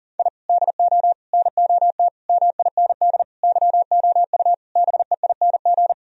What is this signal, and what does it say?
Morse code, heard as a keyed tone, about 37 wpm, 700 Hz. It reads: I DO NOT MIND YOU BEING
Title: Sentences from Top 1000 Words - Review Part 3 - 40wpm